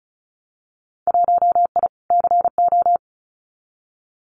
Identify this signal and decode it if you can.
Morse code, heard as a keyed tone, about 36 wpm, 700 Hz. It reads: 1I CO
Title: Winter Field Day Exchanges Encore 35wpm